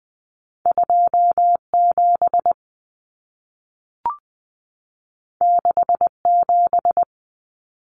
Morse code, heard as a keyed tone, about 20 wpm, 700 Hz. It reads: 27 67
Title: Common Cut Numbers - 2-Digit 20wpm